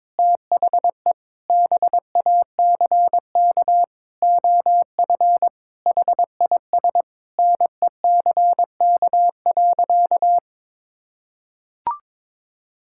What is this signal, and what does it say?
Morse code, heard as a keyed tone, about 22 wpm, 700 Hz. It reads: THE BACK OF HIS NECK.
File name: practice / 22wpm / Sentences with 4-Word Phrases - 22wpm